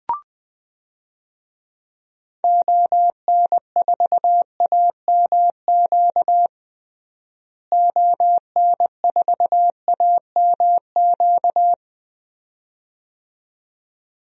Morse code, heard as a keyed tone, about 21 wpm, 700 Hz. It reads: ON4AMQ ON4AMQ
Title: DXCC-100 Call Signs - 6 Characters - Repeated 2x - 20wpm